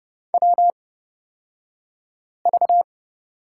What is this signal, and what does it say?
Morse code, heard as a keyed tone, about 29 wpm, spaced as for 5 wpm, 690 Hz. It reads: WV
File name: Lesson 36d - BT - Two Characters 30wpm